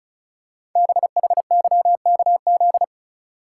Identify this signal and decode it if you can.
Morse code, heard as a keyed tone, about 35 wpm, 700 Hz. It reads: BHYKZ
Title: Code Groups - 5 Letters - 35wpm